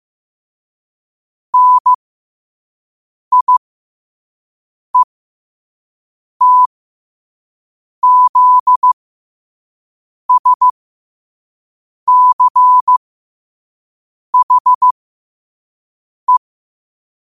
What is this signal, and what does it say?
Morse code, heard as a keyed tone, about 15 wpm, spaced as for 5 wpm, 1000 Hz. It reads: NIETZSCHE
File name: qotdfiles / 5wpm